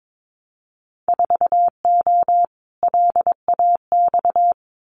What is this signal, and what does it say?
Morse code, heard as a keyed tone, about 22 wpm, 700 Hz. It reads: 4O LAX
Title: Winter Field Day Exchanges Encore 22wpm